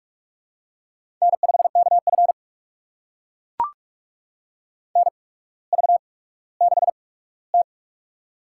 Morse code, heard as a keyed tone, about 44 wpm, spaced as for 19 wpm, 700 Hz. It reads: N5KF NV6T
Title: USA Call Signs - 4 Characters - 8x Character Spacing - 45wpm